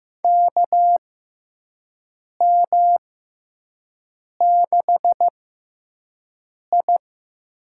Morse code, heard as a keyed tone, about 16 wpm, spaced as for 5 wpm, 700 Hz. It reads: KM6I
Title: USA Call Signs - 6 Characters - 6x Character Spacing - 15wpm